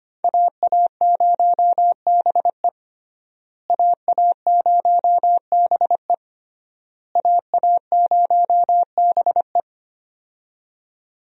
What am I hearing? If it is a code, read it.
Morse code, heard as a keyed tone, about 25 wpm, 700 Hz. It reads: AA0BE AA0BE AA0BE
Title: USA Call Signs - 5 Characters - Repeated 3x - 25wpm